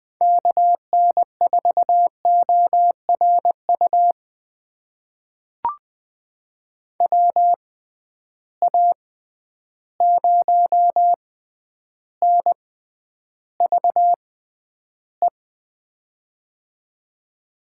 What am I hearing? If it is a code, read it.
Morse code, heard as a keyed tone, about 20 wpm, spaced as for 10 wpm, 700 Hz. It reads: KN4ORU WA0NVE
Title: USA Call Signs - 6 Characters - 6x Character Spacing - 20wpm